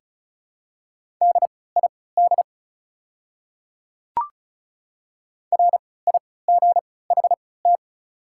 Morse code, heard as a keyed tone, about 34 wpm, spaced as for 20 wpm, 700 Hz. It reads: DID RIGHT
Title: 3x Character Spacing - Top 200 Words - 35wpm